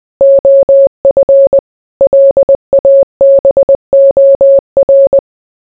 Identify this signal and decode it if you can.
Morse code, heard as a keyed tone, about 20 wpm, 550 Hz. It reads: OF LABOR